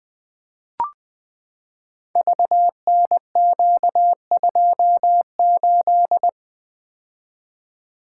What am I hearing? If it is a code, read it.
Morse code, heard as a keyed tone, about 21 wpm, 700 Hz. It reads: VNQ28
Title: Code Groups - 5 Letter:Numbers - 20wpm